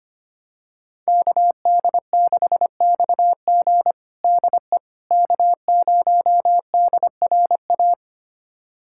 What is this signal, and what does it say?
Morse code, heard as a keyed tone, about 25 wpm, 700 Hz. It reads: KD6XG DE K0DRA
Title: USA Callsign DE Callsign - 5 Characters - 25wpm